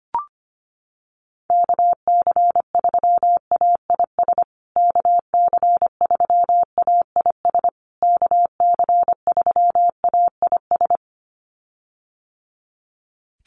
Morse code, heard as a keyed tone, about 25 wpm, 700 Hz. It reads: KC3AIS KC3AIS KC3AIS
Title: USA Call Signs Repeated 3x 25wpm